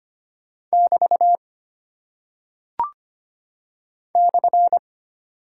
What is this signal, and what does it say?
Morse code, heard as a keyed tone, about 25 wpm, 700 Hz. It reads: <BT> /
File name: Single Punctuation-Prosign - 25wpm